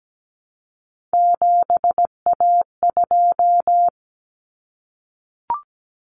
Morse code, heard as a keyed tone, about 17 wpm, 690 Hz. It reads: 7A2
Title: Common Cut Numbers - 3-Digit 17wpm